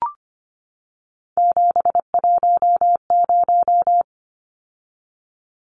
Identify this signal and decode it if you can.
Morse code, heard as a keyed tone, about 25 wpm, 700 Hz. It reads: 710
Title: Numbers - 3-Digit 25wpm